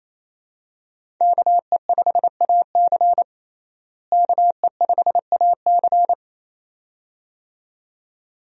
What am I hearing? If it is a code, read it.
Morse code, heard as a keyed tone, about 28 wpm, 700 Hz. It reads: KE5AC KE5AC
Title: USA Call Signs - 5 Characters - Repeated 2x - 28wpm